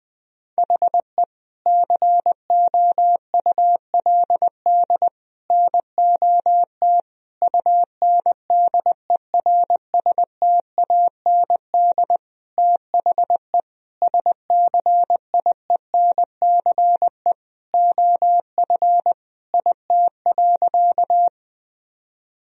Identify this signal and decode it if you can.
Morse code, heard as a keyed tone, about 20 wpm, 700 Hz. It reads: HE COULD NOT UNDERSTAND THE SCIENCE OF IT.
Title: Sentences from Top 1000 Words - Review Part 2 - 20wpm